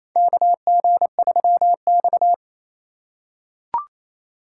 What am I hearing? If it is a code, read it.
Morse code, heard as a keyed tone, about 28 wpm, 700 Hz. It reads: KG3X
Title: USA Call Signs - 4 Characters - 2x Character Spacing - 28wpm